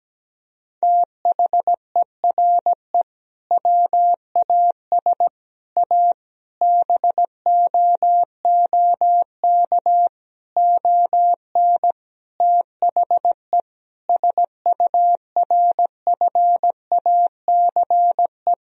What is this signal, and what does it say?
Morse code, heard as a keyed tone, about 17 wpm, 700 Hz. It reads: THERE WAS A BOOK ON THE SURFACE
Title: Sentences from Top 1000 Words - Review Part 3 - 17wpm